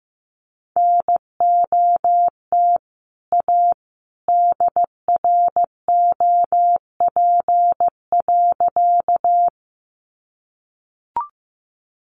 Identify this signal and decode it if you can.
Morse code, heard as a keyed tone, about 15 wpm, 700 Hz. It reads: NOT A DROP.
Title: Sentences from Top 1400 Words 15wpm